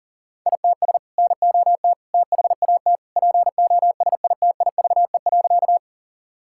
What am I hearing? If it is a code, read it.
Morse code, heard as a keyed tone, about 40 wpm, 700 Hz. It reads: ITS NOT THAT POSITIVE.
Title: Sentences from Top 1000 Words - Review Part 5 - 40wpm